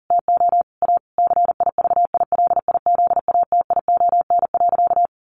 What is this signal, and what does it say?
Morse code, heard as a keyed tone, about 40 wpm, 700 Hz. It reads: TO A CIVILIZATION.
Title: A Sound Of Thunder 40wpm